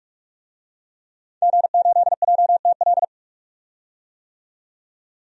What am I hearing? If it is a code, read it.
Morse code, heard as a keyed tone, about 46 wpm, 700 Hz. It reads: G8JTL
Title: DXCC-100 Call Signs - 5 Characters - 45wpm